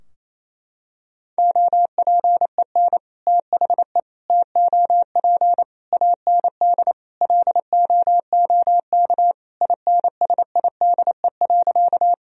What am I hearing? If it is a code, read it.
Morse code, heard as a keyed tone, about 28 wpm, 700 Hz. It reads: OPEN THE TOP AND LOOK INSIDE.
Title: Sentences from Top 1000 Words - Review Part 2 - 28wpm